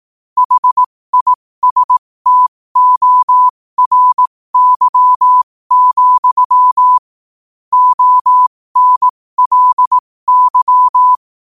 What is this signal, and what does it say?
Morse code, heard as a keyed tone, about 18 wpm, spaced as for 15 wpm, 1000 Hz. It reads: HISTORY, ONLY
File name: qotdfiles / 15wpm